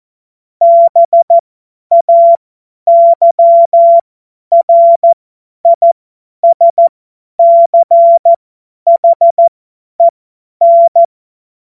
Morse code, heard as a keyed tone, about 14 wpm, spaced as for 10 wpm, 680 Hz. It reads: BAYRISCHEN